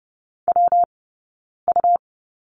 Morse code, heard as a keyed tone, about 30 wpm, 700 Hz. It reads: W U
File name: Lesson 20d - Number 3 - Two Characters 30wpm